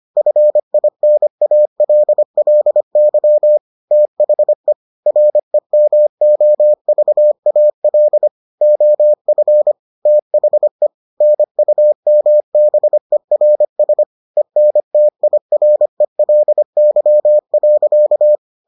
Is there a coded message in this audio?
Yes: Morse code from 0.2 to 18.4 s, heard as a keyed tone, about 25 wpm, 600 Hz. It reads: FINALLY THE REMOVAL OF THE NUMBERS ENTIRELY.